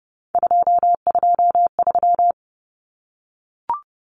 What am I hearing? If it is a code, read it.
Morse code, heard as a keyed tone, about 30 wpm, 700 Hz. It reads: 223
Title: Common Cut Numbers - 3-Digit 30wpm